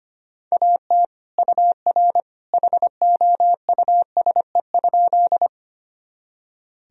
Morse code, heard as a keyed tone, about 25 wpm, 700 Hz. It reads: AT UR HOUSE?
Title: Sentences from Top 200 Words and Abbreviations 25wpm